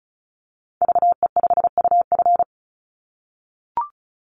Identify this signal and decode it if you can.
Morse code, heard as a keyed tone, about 34 wpm, 700 Hz. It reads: VE5UF
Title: Call Signs - 35wpm